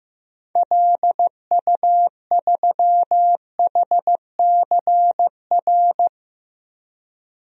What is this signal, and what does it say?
Morse code, heard as a keyed tone, about 15 wpm, 700 Hz. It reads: LU3HCR
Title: DXCC-100 Call Signs - 6 Characters - 15wpm